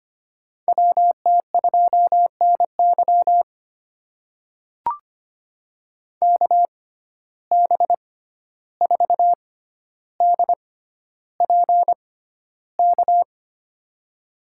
Morse code, heard as a keyed tone, about 25 wpm, spaced as for 12 wpm, 700 Hz. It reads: WT2NY KB4DPK
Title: USA Call Signs - 6x Character Spacing - 25wpm